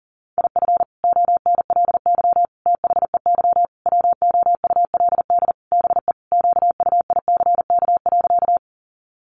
Morse code, heard as a keyed tone, about 40 wpm, 700 Hz. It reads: IF ONLY THEY WOULD BE QUICK.
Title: Sentences from Top 500 Words 40wpm